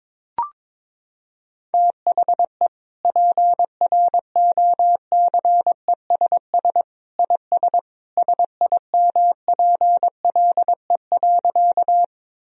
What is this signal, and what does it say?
Morse code, heard as a keyed tone, about 22 wpm, 700 Hz. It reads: THE PROCESS IS SIMPLE.
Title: Sentences from Top 1000 Words - Review Part 3 - 22wpm